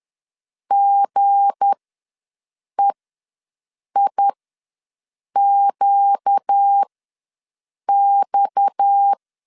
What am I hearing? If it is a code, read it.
Morse code, heard as a keyed tone, about 10 wpm, spaced as for 6 wpm, 800 Hz. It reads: GEIQX
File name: LEZIONI / 9 2 ws 10wpm 4130